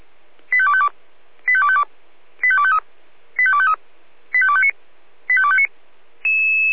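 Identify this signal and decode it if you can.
Five-tone selective call (ZVEI), heard as quick runs of five tones, tones of about 70 ms. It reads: 85252 85252 85253 85253 85259 85259
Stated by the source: ZVEI five-tone call 85264 received on an SDR.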